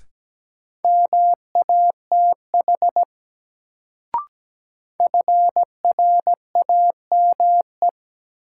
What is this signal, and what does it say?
Morse code, heard as a keyed tone, about 17 wpm, 700 Hz. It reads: MATH FRAME
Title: Sentences from Top 1000 Words - Review Part 6 - New Word Drill - 17wpm